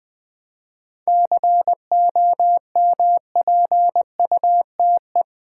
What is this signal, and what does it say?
Morse code, heard as a keyed tone, about 20 wpm, 700 Hz. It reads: COMPUTE
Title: 7 Letter Words - First 3 Letters Given - 20wpm